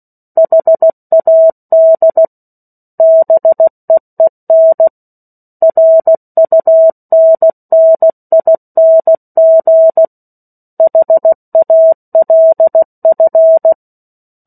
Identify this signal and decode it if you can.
Morse code, heard as a keyed tone, about 16 wpm, 650 Hz. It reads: HAD BEEN RUNNING HALF
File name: 03-Chapter-03 16 WPM